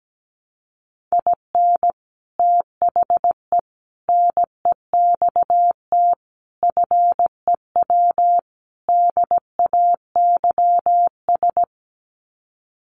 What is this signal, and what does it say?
Morse code, heard as a keyed tone, about 17 wpm, 700 Hz. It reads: IN THE NEXT FEW DAYS
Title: Sets of 5 Words Encore2 17wpm